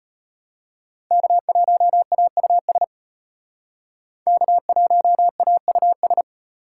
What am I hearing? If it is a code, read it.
Morse code, heard as a keyed tone, about 32 wpm, 700 Hz. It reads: K1AUS K1AUS